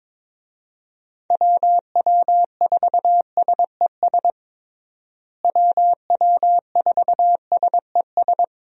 Morse code, heard as a keyed tone, about 22 wpm, 700 Hz. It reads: WW4SES WW4SES
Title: USA Call Signs - 6 Characters - Repeated 2x - 22wpm